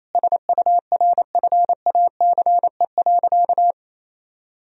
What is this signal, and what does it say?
Morse code, heard as a keyed tone, about 28 wpm, 700 Hz. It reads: SURFACE.
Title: Sentences from Top 1000 Words - Review Part 3 - 28wpm